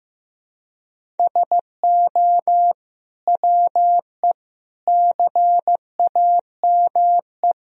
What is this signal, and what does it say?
Morse code, heard as a keyed tone, about 16 wpm, 700 Hz. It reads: SO WE CAME
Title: Sentences from Top 400 Words and Abbreviations 15wpm